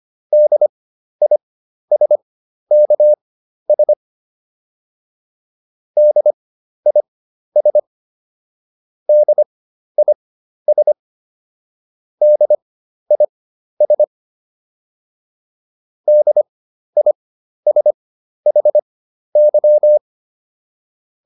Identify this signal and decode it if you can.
Morse code, heard as a keyed tone, about 25 wpm, spaced as for 12 wpm, 600 Hz. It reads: DISKS DIS DIS DIS DISHY